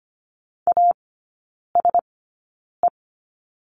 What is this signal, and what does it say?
Morse code, heard as a keyed tone, about 25 wpm, spaced as for 9 wpm, 690 Hz. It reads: ASE
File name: Lesson 15e - Number 5 - Three Characters 25wpm